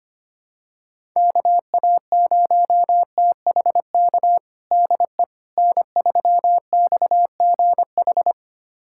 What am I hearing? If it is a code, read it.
Morse code, heard as a keyed tone, about 25 wpm, 700 Hz. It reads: KA0THK DE N3XGH
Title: USA Callsign DE Callsign 25wpm